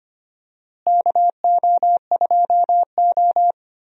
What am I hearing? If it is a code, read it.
Morse code, heard as a keyed tone, about 25 wpm, 700 Hz. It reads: KO2O
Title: USA Call Signs - 4 Characters - Repeated 2x - 25wpm